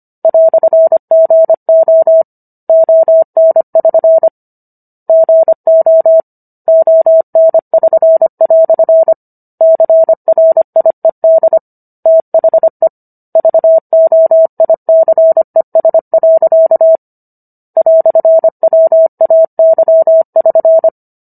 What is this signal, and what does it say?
Morse code, heard as a keyed tone, about 25 wpm, 650 Hz. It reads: "GO ON<SN> GO ON<SN>" CRIED THE VOICES. "WAY<SN>